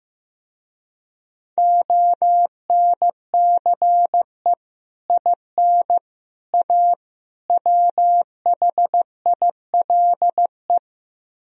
Morse code, heard as a keyed tone, about 16 wpm, 700 Hz. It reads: ONCE IN A WHILE
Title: Speed-Racing - Sets of 4 Words 15wpm